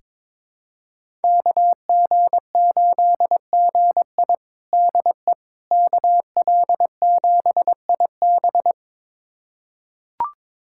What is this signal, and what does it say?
Morse code, heard as a keyed tone, about 22 wpm, 700 Hz. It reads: KG8GI DE KL7IB
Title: USA Callsign DE Callsign 22wpm